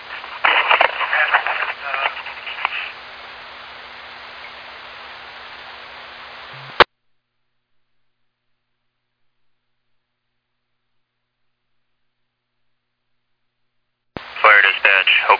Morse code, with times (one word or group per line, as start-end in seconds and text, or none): none